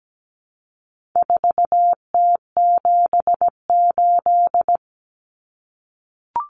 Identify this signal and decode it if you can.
Morse code, heard as a keyed tone, about 17 wpm, 700 Hz. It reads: 4T78
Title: Common Cut Numbers - 4-Digit 17wpm